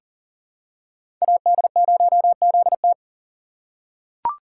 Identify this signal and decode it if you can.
Morse code, heard as a keyed tone, about 40 wpm, 700 Hz. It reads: AD0ZT
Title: USA Call Signs - 4x Character Spacing - 40wpm